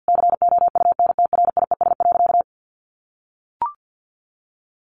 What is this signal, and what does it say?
Morse code, heard as a keyed tone, about 48 wpm, 700 Hz. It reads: COUNTRIES.
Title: A Sound Of Thunder 50wpm